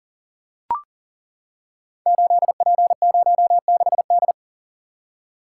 Morse code, heard as a keyed tone, about 39 wpm, 700 Hz. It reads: 8P06D
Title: Code Groups - 5 Letter:Numbers - 40wpm